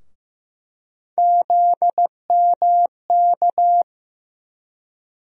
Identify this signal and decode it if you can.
Morse code, heard as a keyed tone, about 15 wpm, 700 Hz. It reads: ZMK
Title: USA Call Sign Postfixes - 3 Letters 15wpm